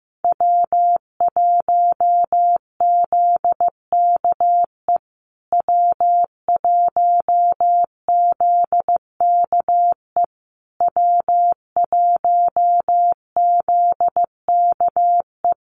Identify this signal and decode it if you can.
Morse code, heard as a keyed tone, about 15 wpm, 700 Hz. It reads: W1ZKE W1ZKE W1ZKE
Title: USA Call Signs Repeated 3x 15wpm